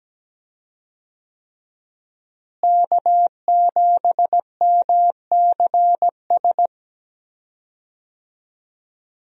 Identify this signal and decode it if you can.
Morse code, heard as a keyed tone, about 17 wpm, 700 Hz. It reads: K7MCS
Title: USA Call Signs - 5 Characters - Repeated 3x - 17wpm